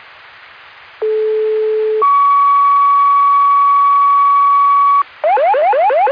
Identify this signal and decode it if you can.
Two-tone page, heard as a one-second tone, then a three-second tone, about 430 then 1130 Hz. It is heard from 1.0 to 5.0 s.